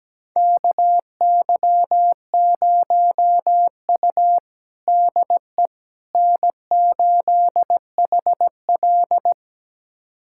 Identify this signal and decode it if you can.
Morse code, heard as a keyed tone, about 17 wpm, 700 Hz. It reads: KY0U DE N8HL
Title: USA Callsign DE Callsign - 4 Characters - 17wpm